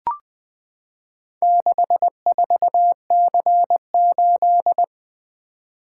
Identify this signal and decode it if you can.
Morse code, heard as a keyed tone, about 20 wpm, 700 Hz. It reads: 64C8
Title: Four Letter-Number Combinations - 20wpm